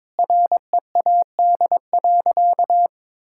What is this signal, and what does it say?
Morse code, heard as a keyed tone, about 22 wpm, 700 Hz. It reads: READ.
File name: practice / 22wpm / Sentences with 4-Word Phrases - 22wpm